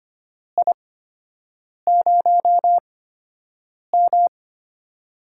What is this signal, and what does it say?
Morse code, heard as a keyed tone, about 25 wpm, spaced as for 7 wpm, 700 Hz. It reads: I0M
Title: USA Call Signs - 8x Character Spacing - 25wpm